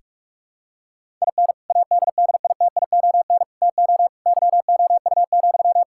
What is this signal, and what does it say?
Morse code, heard as a keyed tone, about 46 wpm, 700 Hz. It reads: IN ADDITION TO YOU,
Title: Sentences from Top 1000 Words - Review Part 6 - 45wpm